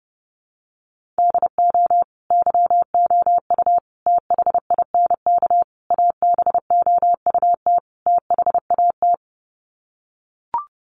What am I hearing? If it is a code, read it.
Morse code, heard as a keyed tone, about 30 wpm, 700 Hz. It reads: DO YOU THINK ABOUT THAT
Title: Sets of 5 Words Encore2 30wpm